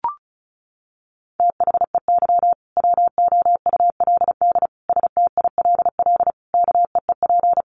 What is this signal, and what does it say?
Morse code, heard as a keyed tone, about 34 wpm, 700 Hz. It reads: THEY WOULD STILL KEEP
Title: Sentences from Top 1400 Words 35wpm